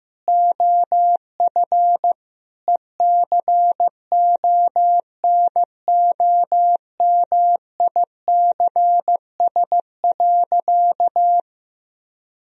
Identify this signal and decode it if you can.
Morse code, heard as a keyed tone, about 15 wpm, 700 Hz. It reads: OF ECONOMICS.
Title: Sentences from Top 1000 Words - Review Part 3 - 15wpm